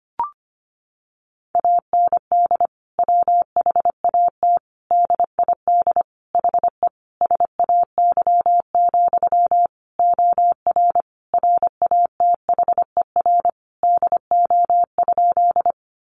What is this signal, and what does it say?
Morse code, heard as a keyed tone, about 25 wpm, 700 Hz. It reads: AND WHAT DID HE SAY, OR RATHER DO?
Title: Sentences from Top 700 Words 25wpm